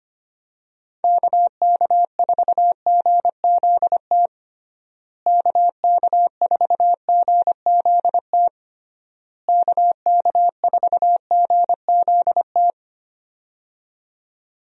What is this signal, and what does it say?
Morse code, heard as a keyed tone, about 25 wpm, 700 Hz. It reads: KK4GZT KK4GZT KK4GZT
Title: USA Call Signs - 6 Characters - Repeated 3x - 25wpm